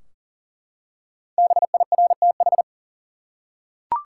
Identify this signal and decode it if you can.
Morse code, heard as a keyed tone, about 40 wpm, 700 Hz. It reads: BIRTH
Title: Five Letter Words - 40wpm